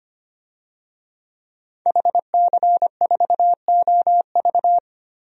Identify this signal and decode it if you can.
Morse code, heard as a keyed tone, about 25 wpm, 700 Hz. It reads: HC4OV
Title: DXCC-100 Call Signs - 5 Characters - 25wpm